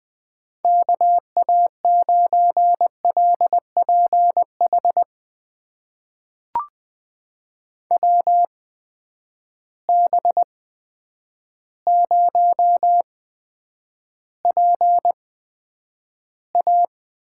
Morse code, heard as a keyed tone, about 20 wpm, spaced as for 9 wpm, 700 Hz. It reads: KA9LPH WB0PA
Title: USA Call Signs - 6 Characters - 8x Character Spacing - 20wpm